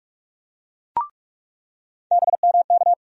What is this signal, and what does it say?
Morse code, heard as a keyed tone, about 43 wpm, 700 Hz. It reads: BMK
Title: USA Call Sign Postfixes - 3 Letters 45wpm